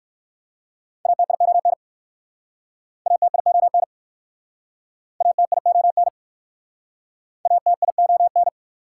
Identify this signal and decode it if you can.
Morse code, heard as a keyed tone, about 63 wpm, 690 Hz. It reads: ATION ATION ATION ATION